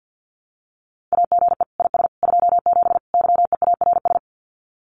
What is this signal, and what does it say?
Morse code, heard as a keyed tone, about 48 wpm, 700 Hz. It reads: AGE IS 27 YEARS
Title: Common QSO Phrases Encore 50wpm